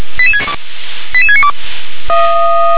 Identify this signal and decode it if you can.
Five-tone selective call (ZVEI), heard as quick runs of five tones, tones of about 70 ms. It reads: 88692